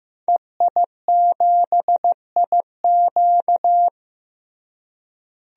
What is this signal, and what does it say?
Morse code, heard as a keyed tone, about 15 wpm, 700 Hz. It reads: EI7IQ
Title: DXCC-100 Call Signs - 5 Characters - Repeated 3x - 15wpm